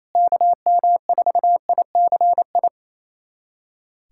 Morse code, heard as a keyed tone, about 28 wpm, 700 Hz. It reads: KM4ICI
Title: USA Callsign DE Callsign 28wpm